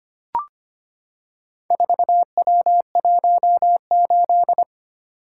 Morse code, heard as a keyed tone, about 25 wpm, 700 Hz. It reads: 4W18
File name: practice / 25wpm / Four Letter-Number Combinations - 25wpm